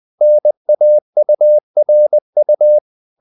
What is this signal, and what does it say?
Morse code, heard as a keyed tone, about 20 wpm, 600 Hz. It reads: NAURU
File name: CW / Practice / NationalCapitals20